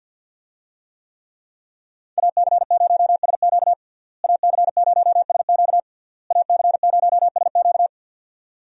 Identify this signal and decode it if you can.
Morse code, heard as a keyed tone, about 52 wpm, 700 Hz. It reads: AC0SQ AC0SQ AC0SQ
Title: USA Call Signs - 5 Characters - Repeated 3x - 50wpm